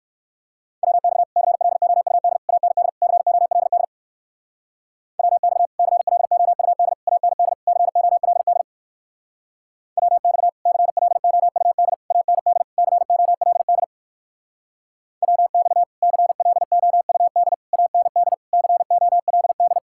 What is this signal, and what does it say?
Morse code, heard as a keyed tone, about 55 wpm, 700 Hz. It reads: WX CLOUD AND COLD WX CLOUD AND COLD WX CLOUD AND COLD WX CLOUD AND COLD
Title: Speed-Racing Common QSO Phrases - 45wpm